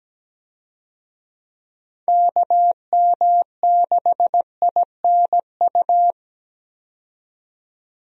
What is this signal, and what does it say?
Morse code, heard as a keyed tone, about 17 wpm, 700 Hz. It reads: KM6INU
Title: USA Call Signs - 6 Characters - 17wpm